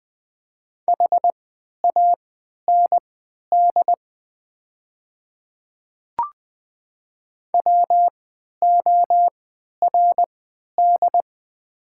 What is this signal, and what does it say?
Morse code, heard as a keyed tone, about 20 wpm, spaced as for 11 wpm, 700 Hz. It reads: HAND WORD
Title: 3x Character Spacing - Top 200 Words - 20wpm